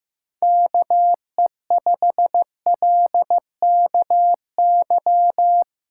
Morse code, heard as a keyed tone, about 16 wpm, 700 Hz. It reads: KE5LKY
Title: USA Callsign DE Callsign - 6 Characters - 15wpm